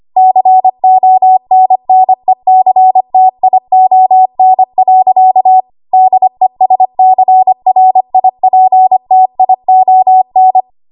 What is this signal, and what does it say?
Morse code, heard as a keyed tone, about 25 wpm, 750 Hz. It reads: CONNECTION. DESCRIPTION
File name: Morse / 25WPM